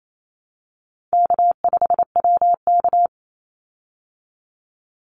Morse code, heard as a keyed tone, about 28 wpm, 700 Hz. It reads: K5WK
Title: Easy Call Signs Repeated 3x - 28wpm